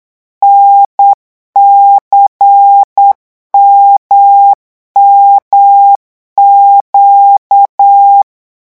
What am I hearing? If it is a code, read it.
Morse code, heard as a keyed tone, about 8 wpm, 800 Hz. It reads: NCMMQ
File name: morse mp3 / revision3-vitesse 8 mots